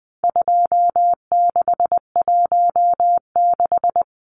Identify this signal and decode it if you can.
Morse code, heard as a keyed tone, about 20 wpm, 700 Hz. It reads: 2616
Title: Numbers - 4-Digit 20wpm